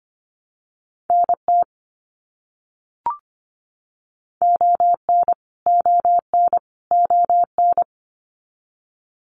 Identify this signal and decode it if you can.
Morse code, heard as a keyed tone, about 25 wpm, 700 Hz. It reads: NT ON ON ON
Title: Canadian Provinces and Territories - Repeated 3x - 25wpm